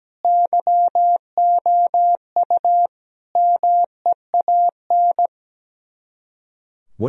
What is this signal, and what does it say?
Morse code, heard as a keyed tone, about 17 wpm, 700 Hz. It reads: YOU MEAN
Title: Speed-Racing - Sets of 4 Words 17wpm